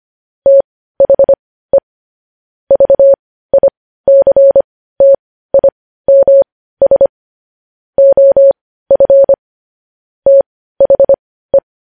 Morse code, heard as a keyed tone, about 25 wpm, spaced as for 15 wpm, 550 Hz. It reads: THE VICTIMS OF THE